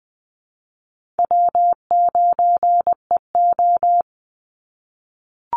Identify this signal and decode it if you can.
Morse code, heard as a keyed tone, about 20 wpm, 700 Hz. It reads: W9EO
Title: Easy Call Signs - 20wpm